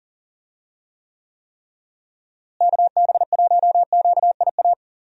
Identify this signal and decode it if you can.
Morse code, heard as a keyed tone, about 39 wpm, 700 Hz. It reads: KB1QIA
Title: USA Call Signs - 6 Characters - Repeated 3x - 40wpm